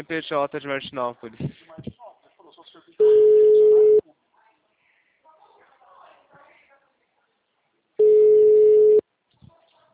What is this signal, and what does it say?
Ringback tone (Europe, 425 Hz), with speech.